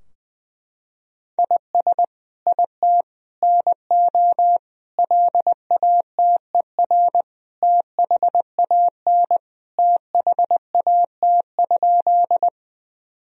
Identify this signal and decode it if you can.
Morse code, heard as a keyed tone, about 20 wpm, 700 Hz. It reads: IS IT NO LATER THAN THAT?
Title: Sentences from Top 1000 Words - Review Part 4 - 20wpm